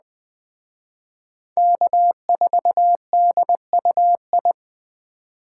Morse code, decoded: K4DUI

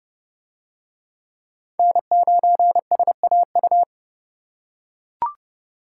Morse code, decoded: N9SAU

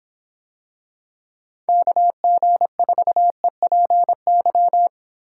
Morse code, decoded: KG4EPY

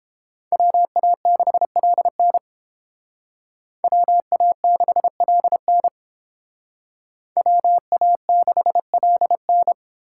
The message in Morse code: WA6LN WA6LN WA6LN

